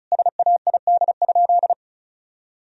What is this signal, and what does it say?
Morse code, heard as a keyed tone, about 36 wpm, 700 Hz. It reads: SAID?
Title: Sentences from Top 1000 Words - Review Part 1 - 35wpm